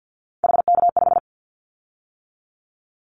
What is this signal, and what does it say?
Morse code, heard as a keyed tone, about 50 wpm, 700 Hz. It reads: HK5
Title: Callsign Prefixes for Top 100 DXCC - 50wpm